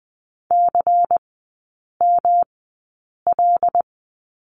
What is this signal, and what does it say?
Morse code, heard as a keyed tone, about 20 wpm, spaced as for 8 wpm, 700 Hz. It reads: CML
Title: 107 Lesson 18e - Letter m - Three Characters 20wpm